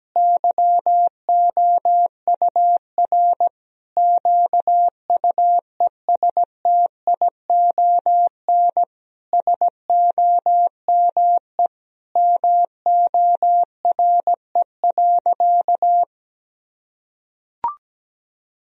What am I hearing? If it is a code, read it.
Morse code, heard as a keyed tone, about 17 wpm, 700 Hz. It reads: YOUR QUESTION SOME MORE.
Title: Sentences from Top 1000 Words - Review Part 2 - 17wpm